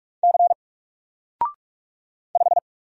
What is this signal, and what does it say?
Morse code, heard as a keyed tone, about 44 wpm, 700 Hz. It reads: C 5
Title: Single Letter-Number 45wpm